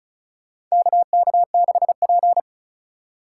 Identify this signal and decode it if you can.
Morse code, heard as a keyed tone, about 34 wpm, 700 Hz. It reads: KK6P
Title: USA Call Signs - 4 Characters - 2x Character Spacing - 35wpm